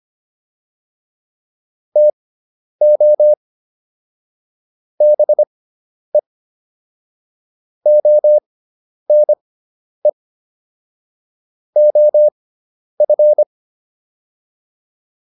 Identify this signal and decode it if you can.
Morse code, heard as a keyed tone, about 25 wpm, spaced as for 10 wpm, 600 Hz. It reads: TO BE ONE OF